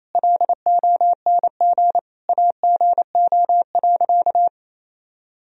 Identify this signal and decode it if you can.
Morse code, heard as a keyed tone, about 28 wpm, 700 Hz. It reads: LONG AGO.
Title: Sentences from Top 1000 Words - Review Part 3 - 28wpm